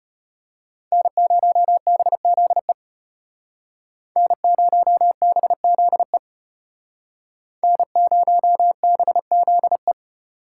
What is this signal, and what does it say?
Morse code, heard as a keyed tone, about 33 wpm, 700 Hz. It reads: N0BZE N0BZE N0BZE